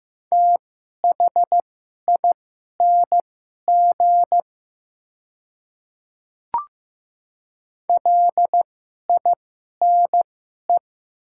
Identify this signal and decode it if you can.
Morse code, heard as a keyed tone, about 15 wpm, spaced as for 11 wpm, 700 Hz. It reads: THING LINE
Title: 2x Character Spacing - Top 200 Words - 15wpm